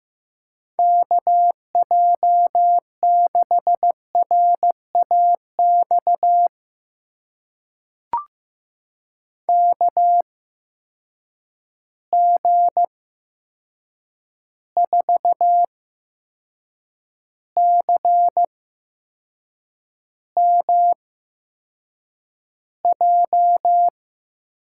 Morse code, heard as a keyed tone, about 16 wpm, spaced as for 6 wpm, 700 Hz. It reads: KJ6RAX KG4CMJ
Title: USA Call Signs - 6 Characters - 8x Character Spacing - 15wpm